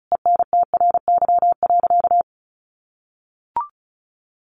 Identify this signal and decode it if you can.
Morse code, heard as a keyed tone, about 34 wpm, 700 Hz. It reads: ENTRY.
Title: Sentences from Top 1500 Words 35wpm